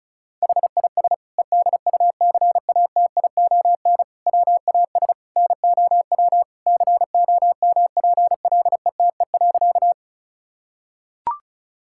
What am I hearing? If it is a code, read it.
Morse code, heard as a keyed tone, about 36 wpm, 700 Hz. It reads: HIS EDUCATION WAS NOW COMPLETE.
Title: Sentences from Top 1000 Words - Review Part 6 - 35wpm